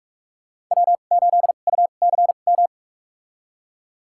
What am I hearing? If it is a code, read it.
Morse code, heard as a keyed tone, about 46 wpm, spaced as for 33 wpm, 700 Hz. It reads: W8UCM
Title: USA Call Signs - 5 Characters - 2x Character Spacing - 45wpm